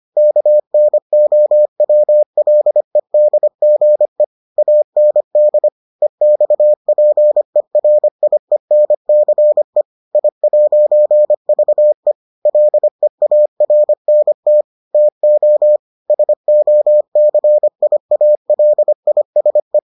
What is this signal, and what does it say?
Morse code, heard as a keyed tone, about 25 wpm, 600 Hz. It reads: KNOWLEDGE AND EXPERIENCE I'VE LEARNT TO SOCIALISE